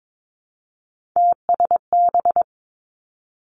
Morse code, heard as a keyed tone, about 22 wpm, 700 Hz. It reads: TSB
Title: Three Challenging Letter-Number Combinations 22wpm